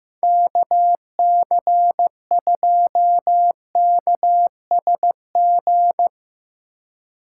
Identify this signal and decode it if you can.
Morse code, heard as a keyed tone, about 15 wpm, 700 Hz. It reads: KC2KSG
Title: USA Call Signs - 6 Characters - 2x Character Spacing - 15wpm